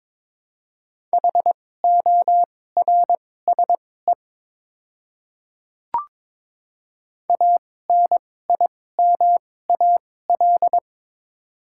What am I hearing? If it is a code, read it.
Morse code, heard as a keyed tone, about 22 wpm, spaced as for 16 wpm, 700 Hz. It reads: HORSE ANIMAL
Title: 2x Character Spacing - Top 300 Words - 22wpm